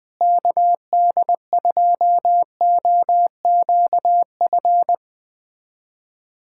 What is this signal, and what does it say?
Morse code, heard as a keyed tone, about 20 wpm, 700 Hz. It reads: KD2OQF
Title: USA Callsign DE Callsign - 6 Characters - 20wpm